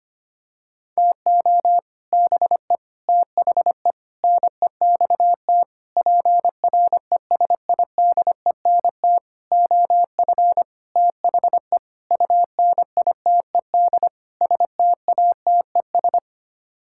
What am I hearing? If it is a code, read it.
Morse code, heard as a keyed tone, about 25 wpm, 700 Hz. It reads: TO BE THE NEXT PRESIDENT OF THE UNITED STATES
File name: Sets of 9 Words - 25wpm